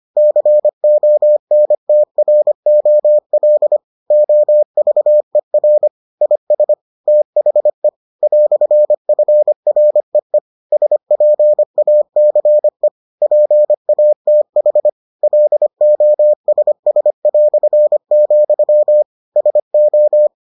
Morse code, heard as a keyed tone, about 25 wpm, 600 Hz. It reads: CONTROL OVER IS THE "FREE SPACE PATH LOSS", SO